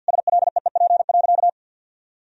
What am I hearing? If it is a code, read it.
Morse code, heard as a keyed tone, about 52 wpm, 700 Hz. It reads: SLEEP.
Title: Sentences from Top 1000 Words - Review Part 2 - 50wpm